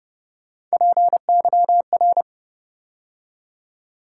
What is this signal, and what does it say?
Morse code, heard as a keyed tone, about 29 wpm, 700 Hz. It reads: PYR